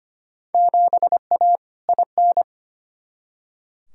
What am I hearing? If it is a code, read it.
Morse code, heard as a keyed tone, about 25 wpm, 700 Hz. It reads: 7A IN
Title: Field Day Exchanges Encore - 25wpm